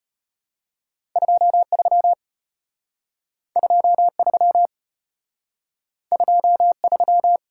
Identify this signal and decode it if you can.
Morse code, heard as a keyed tone, about 33 wpm, 700 Hz. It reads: 23 23 23